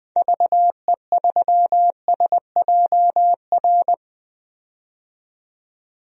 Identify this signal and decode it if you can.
Morse code, heard as a keyed tone, about 20 wpm, 700 Hz. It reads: VE3SJR
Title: DXCC-100 Call Signs - 6 Characters - 20wpm